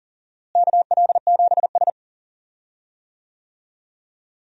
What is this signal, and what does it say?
Morse code, heard as a keyed tone, about 40 wpm, 700 Hz. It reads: KL7S
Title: DXCC-100 Call Signs - 4 Characters - Repeated 2x - 40wpm